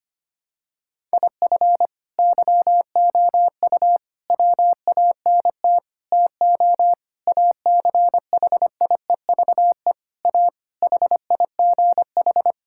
Morse code, heard as a keyed tone, about 25 wpm, 700 Hz. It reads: IF YOU WANT TO ACHIEVE A HIGH
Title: Sentences from Top 1000 Words - Review Part 6 - 25wpm